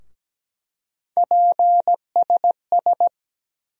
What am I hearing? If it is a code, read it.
Morse code, heard as a keyed tone, about 17 wpm, 700 Hz. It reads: PSS
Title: USA Call Sign Postfixes - 3 Letters 17wpm